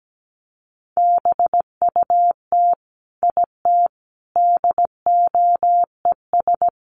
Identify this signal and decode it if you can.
Morse code, heard as a keyed tone, about 17 wpm, 700 Hz. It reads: BUT IT DOES
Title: Sentences from Top 1200 Words 17wpm